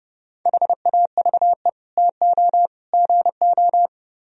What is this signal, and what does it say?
Morse code, heard as a keyed tone, about 30 wpm, 700 Hz. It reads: HAVE TO GO